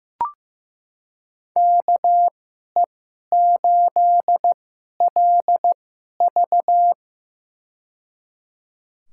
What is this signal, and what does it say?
Morse code, heard as a keyed tone, about 15 wpm, spaced as for 11 wpm, 700 Hz. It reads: KE8LV
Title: USA Call Signs - 5 Characters - 2x Character Spacing - 15wpm